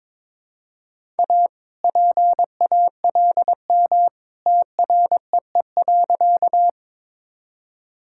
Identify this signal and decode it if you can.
Morse code, heard as a keyed tone, about 21 wpm, 700 Hz. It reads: A PALM TREE.
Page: morse-code-ninja-practice-22wpm